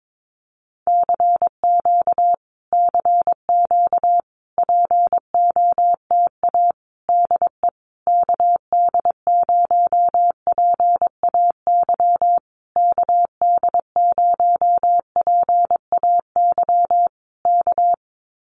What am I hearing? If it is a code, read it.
Morse code, heard as a keyed tone, about 22 wpm, 700 Hz. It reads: CQ CQ POTA DE KD0PAY KD0PAY K
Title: Common QSO Phrases Encore 22wpm